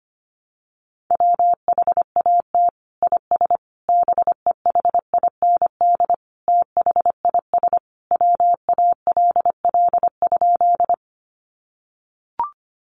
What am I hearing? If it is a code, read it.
Morse code, heard as a keyed tone, about 25 wpm, 700 Hz. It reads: WHAT IS BEHIND THIS WALL?
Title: Sentences from Top 1100 Words 25wpm